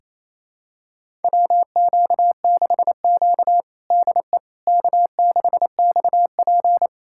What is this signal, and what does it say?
Morse code, heard as a keyed tone, about 27 wpm, 700 Hz. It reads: WQ6Q DE K6XP
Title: USA Callsign DE Callsign - 4 Characters - 28wpm